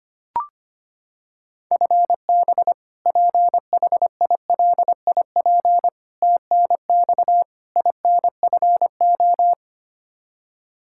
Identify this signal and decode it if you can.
Morse code, heard as a keyed tone, about 25 wpm, 700 Hz. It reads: FB PHILIP TNX INFO
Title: Practice QSO - Exchange 3 - More Info - 25wpm